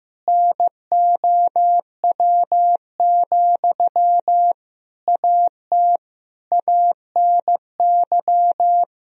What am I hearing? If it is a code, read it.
Morse code, heard as a keyed tone, about 15 wpm, 700 Hz. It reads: NOW, AT ANY
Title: Sentences from Top 1000 Words - Review Part 4 - 15wpm